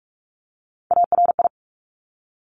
Morse code, heard as a keyed tone, about 46 wpm, 700 Hz. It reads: ARI
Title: Top Three Letter Combinations in Words 45wpm